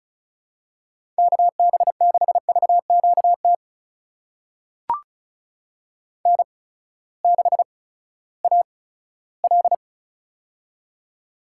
Morse code, heard as a keyed tone, about 36 wpm, spaced as for 18 wpm, 700 Hz. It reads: KB6VQT N6AL
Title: USA Call Signs - 8x Character Spacing - 35wpm